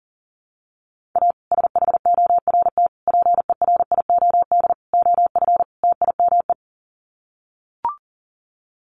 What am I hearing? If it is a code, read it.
Morse code, heard as a keyed tone, about 40 wpm, 700 Hz. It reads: A SHORT PERIOD OF TIME
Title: Sets of 5 Words Encore 40wpm